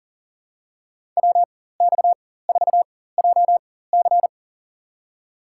Morse code, heard as a keyed tone, about 40 wpm, spaced as for 19 wpm, 700 Hz. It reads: WX4JC
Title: USA Call Signs - 5 Characters - 4x Character Spacing - 40wpm